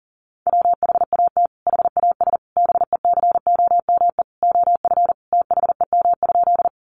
Morse code, heard as a keyed tone, about 40 wpm, 700 Hz. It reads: WHAT HAS BECOME OF THEM?
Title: Sentences from Top 600 Words 40wpm